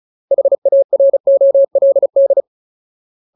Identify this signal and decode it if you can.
Morse code, heard as a keyed tone, about 34 wpm, 550 Hz. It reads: HAROLD
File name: Ultimate Word List / 35WPM Ultimate